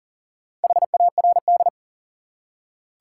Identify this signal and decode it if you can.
Morse code, heard as a keyed tone, about 40 wpm, 700 Hz. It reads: HARD
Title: Four Letter Words - 40wpm